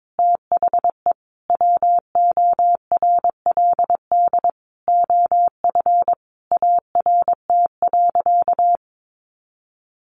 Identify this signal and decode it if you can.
Morse code, heard as a keyed tone, about 22 wpm, 700 Hz. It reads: THE WORLD OF ART.
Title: Sentences from Top 1500 Words 22wpm